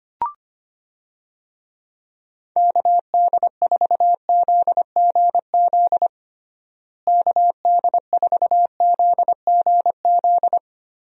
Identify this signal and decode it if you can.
Morse code, heard as a keyed tone, about 25 wpm, 700 Hz. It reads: KD4ZGZ KD4ZGZ
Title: USA Call Signs - 6 Characters - Repeated 3x - 25wpm